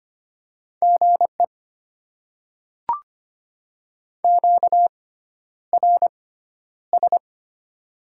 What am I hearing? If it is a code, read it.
Morse code, heard as a keyed tone, about 25 wpm, spaced as for 11 wpm, 700 Hz. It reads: GE QRS
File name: Top 90 QSO Elements - 6x Character Spacing - 25wpm